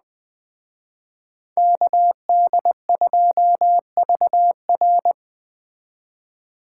Morse code, heard as a keyed tone, about 20 wpm, 700 Hz. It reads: KD2VR
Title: Speed Racing - USA 2x2 Call Signs 17wpm